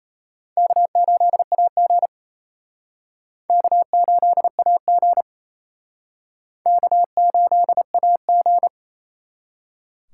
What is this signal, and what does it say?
Morse code, heard as a keyed tone, about 34 wpm, 700 Hz. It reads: K8AG K8AG K8AG
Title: Speed Racing - USA 1x2 Call Signs 28wpm